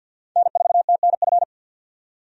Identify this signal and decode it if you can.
Morse code, heard as a keyed tone, about 51 wpm, 700 Hz. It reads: N4TNF
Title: USA Call Signs - 5 Characters - 50wpm